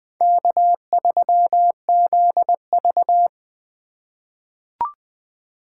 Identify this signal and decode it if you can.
Morse code, heard as a keyed tone, about 20 wpm, 700 Hz. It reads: K3ZV
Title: USA Call Signs - 4 Characters - 4x Character Spacing - 20wpm